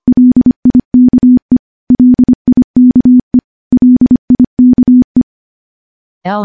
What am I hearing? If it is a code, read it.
Morse code, heard as a keyed tone, about 25 wpm, 250 Hz. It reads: LIKE LIKE LIKE
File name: morse / morse-training-25WPM